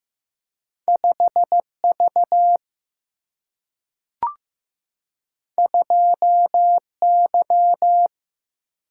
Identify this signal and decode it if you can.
Morse code, heard as a keyed tone, about 15 wpm, 700 Hz. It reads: 5 V 2 Y
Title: Single-Letter Single-Number 15wpm